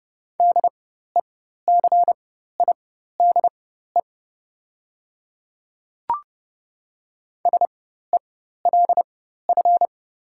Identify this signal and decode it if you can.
Morse code, heard as a keyed tone, about 30 wpm, spaced as for 14 wpm, 690 Hz. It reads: DECIDE SELF
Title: 4x Character Spacing - Top 500 Words - 30wpm